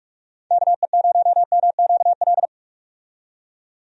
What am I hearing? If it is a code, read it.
Morse code, heard as a keyed tone, about 43 wpm, 700 Hz. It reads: KE0MQL
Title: USA Call Signs - 6 Characters - 45wpm